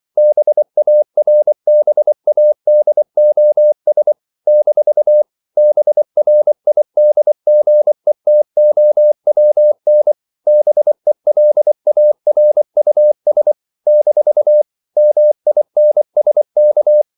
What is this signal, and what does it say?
Morse code, heard as a keyed tone, about 24 wpm, 600 Hz. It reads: BARBADOS - BRIDGETOWN BELARUS - MINSK
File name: CW / Practice / NationalCapitals24